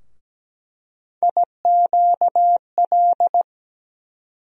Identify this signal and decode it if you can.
Morse code, heard as a keyed tone, about 17 wpm, 700 Hz. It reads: IQL
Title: USA Call Sign Postfixes - 3 Letters 17wpm